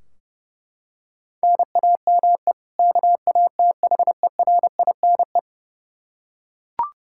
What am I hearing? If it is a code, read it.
Morse code, heard as a keyed tone, about 32 wpm, 700 Hz. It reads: NAME KATHERINE
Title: Practice QSO - Exchange 1 - Beginning - 30wpm